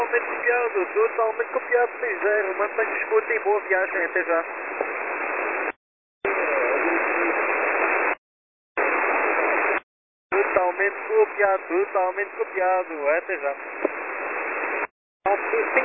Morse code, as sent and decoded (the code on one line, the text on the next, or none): none